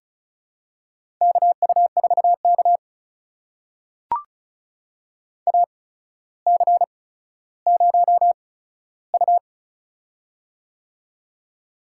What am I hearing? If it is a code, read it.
Morse code, heard as a keyed tone, about 36 wpm, spaced as for 15 wpm, 700 Hz. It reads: KU4K AC0U
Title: USA Call Signs - 4 Characters - 8x Character Spacing - 35wpm